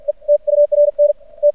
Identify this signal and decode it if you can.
Morse code, heard as a keyed tone, about 51 wpm, spaced as for 45 wpm, 590 Hz. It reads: E TURN T
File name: archived / CRSnet